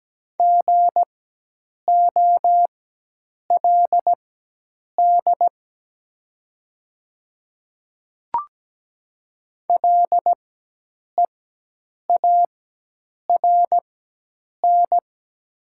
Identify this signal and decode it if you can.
Morse code, heard as a keyed tone, about 17 wpm, spaced as for 8 wpm, 700 Hz. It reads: GOLD LEARN
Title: 4x Character Spacing - Top 500 Words - 17wpm